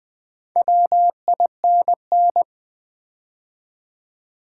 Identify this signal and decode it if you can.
Morse code, heard as a keyed tone, about 20 wpm, 700 Hz. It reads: WINN
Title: Build A Word - 2 Letters At A Time - 6-Letter Words - 20wpm